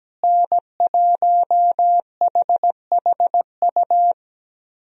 Morse code, heard as a keyed tone, about 17 wpm, 700 Hz. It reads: N1HHU
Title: USA Call Signs - 5 Characters - Repeated 3x - 17wpm